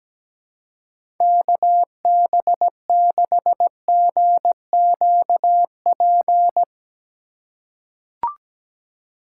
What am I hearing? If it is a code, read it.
Morse code, heard as a keyed tone, about 17 wpm, 700 Hz. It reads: KB6GQP
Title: USA Call Signs - 6 Characters - 3x Character Spacing - 17wpm